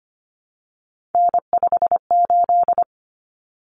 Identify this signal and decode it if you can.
Morse code, heard as a keyed tone, about 25 wpm, 700 Hz. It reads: N58
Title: Common Cut Numbers - 3-Digit 25wpm